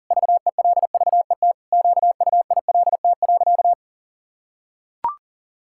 Morse code, heard as a keyed tone, about 40 wpm, 700 Hz. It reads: VELVET QUILT.
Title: Pangram Challenge v2 - 40wpm